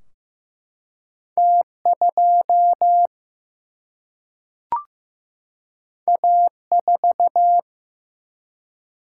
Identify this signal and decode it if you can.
Morse code, heard as a keyed tone, about 15 wpm, 700 Hz. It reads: T2 A4
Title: Single-Letter Single-Number 15wpm